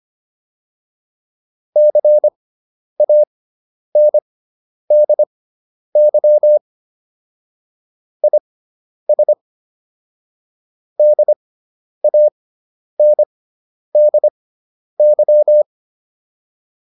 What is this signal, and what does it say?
Morse code, heard as a keyed tone, about 25 wpm, spaced as for 10 wpm, 600 Hz. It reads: CANDY IS DANDY